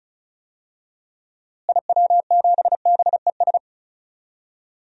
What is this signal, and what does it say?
Morse code, heard as a keyed tone, about 36 wpm, 700 Hz. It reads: IW7BES
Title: DXCC-100 Call Signs - 6 Characters - Repeated 3x - 35wpm